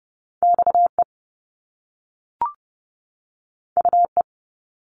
Morse code, heard as a keyed tone, about 30 wpm, 700 Hz. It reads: XE UE
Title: Top Two Letter Combinations in Words 30wpm